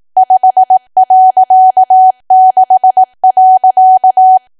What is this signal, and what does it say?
Morse code, heard as a keyed tone, about 18 wpm, 750 Hz. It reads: 5.6.
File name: cwa / W1AW-Level2 / 18WPM